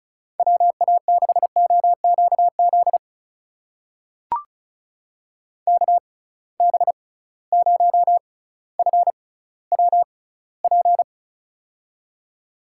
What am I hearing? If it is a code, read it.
Morse code, heard as a keyed tone, about 36 wpm, spaced as for 18 wpm, 700 Hz. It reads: WA6OQZ KB0FWP
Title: USA Call Signs - 6 Characters - 6x Character Spacing - 35wpm